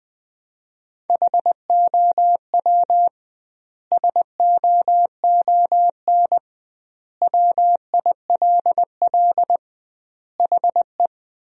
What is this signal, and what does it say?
Morse code, heard as a keyed tone, about 20 wpm, 700 Hz. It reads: HOW SOON WILL HE
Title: 2x Word-Spacing - Sentences from Top 400 Words - 20wpm